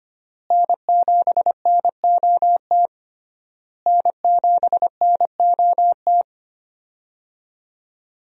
Morse code, heard as a keyed tone, about 25 wpm, 700 Hz. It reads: N7NOT N7NOT
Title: USA Call Signs - 5 Characters - Repeated 3x - 25wpm